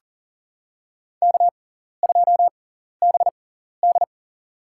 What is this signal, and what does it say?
Morse code, heard as a keyed tone, about 40 wpm, spaced as for 14 wpm, 700 Hz. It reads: K2BD
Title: USA Call Signs - 4 Characters - 6x Character Spacing - 40wpm